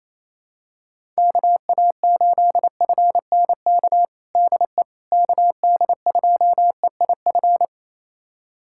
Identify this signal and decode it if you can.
Morse code, heard as a keyed tone, about 28 wpm, 700 Hz. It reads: KA8FNK DE KD2EIF
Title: USA Callsign DE Callsign - 6 Characters - 28wpm